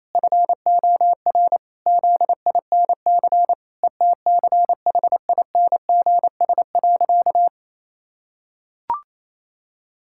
Morse code, heard as a keyed tone, about 28 wpm, 700 Hz. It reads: FOR ZINC ETCHINGS.
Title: Pangram Challenge v2 - 28wpm